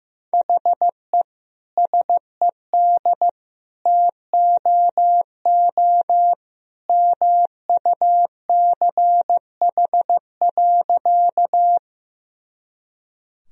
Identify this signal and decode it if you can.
Morse code, heard as a keyed tone, about 15 wpm, 700 Hz. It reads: HE SED TOO MUCH.
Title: Sentences from Top 200 Words and Abbreviations 15wpm